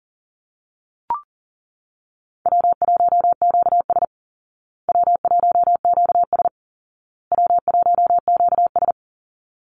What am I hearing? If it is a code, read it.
Morse code, heard as a keyed tone, about 40 wpm, 700 Hz. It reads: W1QS W1QS W1QS
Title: Easy Call Signs Repeated 3x - 40wpm